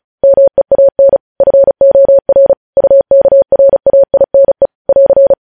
Morse code, heard as a keyed tone, about 36 wpm, 550 Hz. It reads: MEAN FOR UKRAINE <AR>
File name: podcast / News headlines-1x-35x35